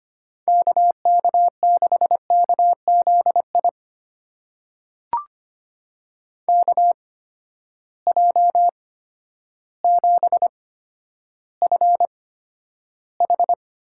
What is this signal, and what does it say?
Morse code, heard as a keyed tone, about 25 wpm, spaced as for 11 wpm, 700 Hz. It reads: KK6KZI KJ7FH
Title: USA Call Signs - 6 Characters - 8x Character Spacing - 25wpm